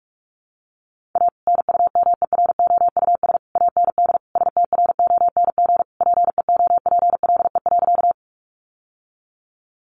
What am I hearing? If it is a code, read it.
Morse code, heard as a keyed tone, about 44 wpm, 700 Hz. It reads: A NUMEROUS AND STRONG PEOPLE.
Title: Sentences from Top 1400 Words 45wpm